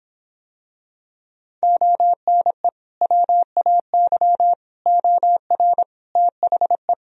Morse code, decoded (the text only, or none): ONE WAY OR THE